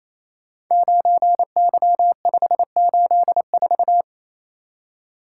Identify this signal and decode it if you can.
Morse code, heard as a keyed tone, about 27 wpm, 700 Hz. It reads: 9Y584
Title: Code Groups - 5 Letter:Numbers - 28wpm